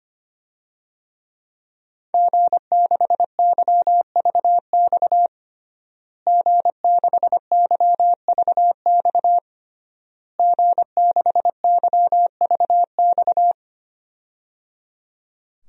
Morse code, heard as a keyed tone, about 25 wpm, 700 Hz. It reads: G6YVX G6YVX G6YVX
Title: DXCC-100 Call Signs - 5 Characters - Repeated 3x - 25wpm